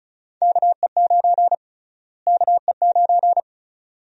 Morse code, heard as a keyed tone, about 34 wpm, 700 Hz. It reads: KE9 KE9
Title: USA Call Sign Prefixes - Repeated 2x - 35wpm